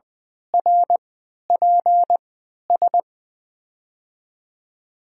Morse code, heard as a keyed tone, about 21 wpm, spaced as for 12 wpm, 700 Hz. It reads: RPS
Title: USA Call Signs - 6 Characters - 3x Character Spacing - 20wpm